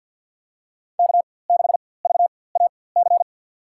Morse code, heard as a keyed tone, about 53 wpm, spaced as for 24 wpm, 700 Hz. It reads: K6VAC
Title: USA Call Signs - 4x Character Spacing - 50wpm